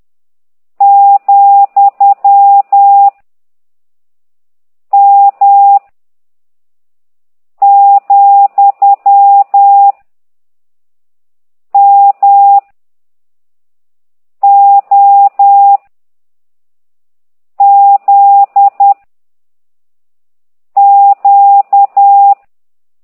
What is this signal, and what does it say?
Morse code, heard as a keyed tone, about 10 wpm, 800 Hz. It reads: , M , M O Z Q